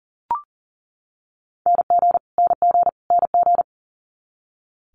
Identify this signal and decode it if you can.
Morse code, heard as a keyed tone, about 37 wpm, 700 Hz. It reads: NG NG NG
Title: Multi-letter Phonemes Repeated 3x 40wpm